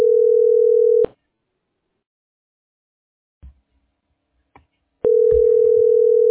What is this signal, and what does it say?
Ringback tone (North America), 440 + 480 Hz.